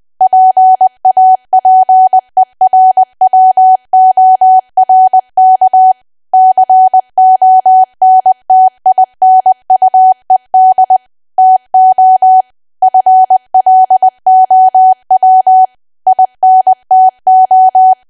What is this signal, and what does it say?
Morse code, heard as a keyed tone, about 20 wpm, 750 Hz. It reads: PAPERWORK CONTINUED TO FLOW INTO
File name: cwa / W1AW-Level3 / 20WPM